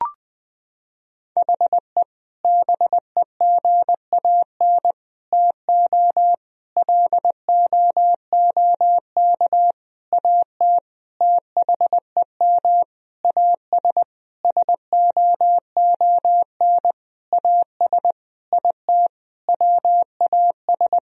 Morse code, heard as a keyed tone, about 20 wpm, 700 Hz. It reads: HE BEGAN TO LOOK AT THEM AS SOON AS IT WAS
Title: Sentences from Top 1000 Words - Review Part 1 - 20wpm